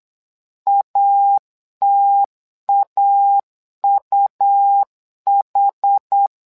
Morse code, heard as a keyed tone, about 8 wpm, 800 Hz. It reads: ATAUH
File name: LEZIONI / 2 UHDN / 2 UH 8wpm 2129